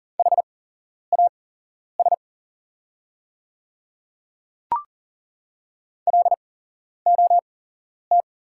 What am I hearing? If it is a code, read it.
Morse code, heard as a keyed tone, about 40 wpm, spaced as for 11 wpm, 690 Hz. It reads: HAS LOT
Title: 8x Character Spacing - Top 500 Words - 40wpm